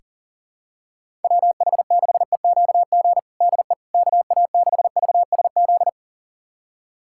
Morse code, heard as a keyed tone, about 40 wpm, 700 Hz. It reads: WH6EQG DE KA6VSZ
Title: USA Callsign DE Callsign - 6 Characters - 40wpm